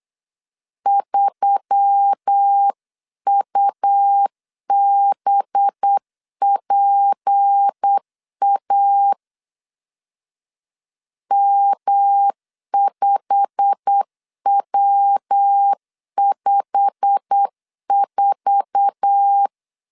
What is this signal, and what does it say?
Morse code, heard as a keyed tone, about 8 wpm, 800 Hz. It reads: 3UBPA M5W54